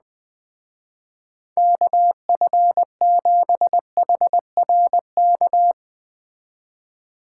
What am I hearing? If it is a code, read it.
Morse code, heard as a keyed tone, about 20 wpm, 700 Hz. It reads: KF7HRK
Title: Speed Racing - USA 2x3 Call Signs 17wpm